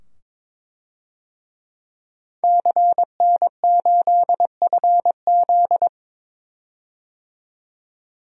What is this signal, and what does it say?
Morse code, heard as a keyed tone, about 22 wpm, 700 Hz. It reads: CN8FZ
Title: DXCC-100 Call Signs - 5 Characters - 22wpm